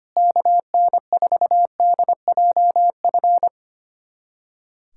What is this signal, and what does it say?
Morse code, heard as a keyed tone, about 25 wpm, 700 Hz. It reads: KN4DJF
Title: USA Callsign DE Callsign - 6 Characters - 25wpm